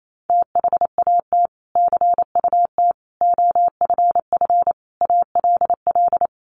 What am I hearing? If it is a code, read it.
Morse code, heard as a keyed tone, about 28 wpm, 700 Hz. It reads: THAT CUT OFF ALL
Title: Sentences from Top 1100 Words 28wpm